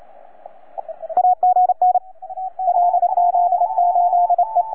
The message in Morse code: AGN LZ8E